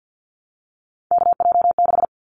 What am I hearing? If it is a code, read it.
Morse code, heard as a keyed tone, about 50 wpm, 700 Hz. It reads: KJ6
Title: Callsign Prefixes for Top 25 DXCC - 50wpm